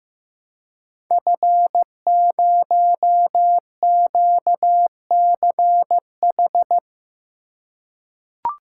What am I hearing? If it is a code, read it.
Morse code, heard as a keyed tone, about 15 wpm, 700 Hz. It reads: F0QCH
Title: Code Groups - 5 Letter:Numbers - 15wpm